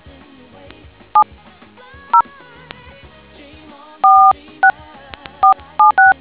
DTMF, heard as telephone touch tones, tones of about 120 ms. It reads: 70 46476